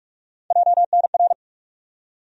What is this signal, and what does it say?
Morse code, heard as a keyed tone, about 43 wpm, 700 Hz. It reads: JNR
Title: USA Call Sign Postfixes - 3 Letters 45wpm